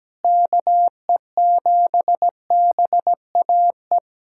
Morse code, heard as a keyed tone, about 17 wpm, 700 Hz. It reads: KE7BAE
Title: USA Callsign DE Callsign 17wpm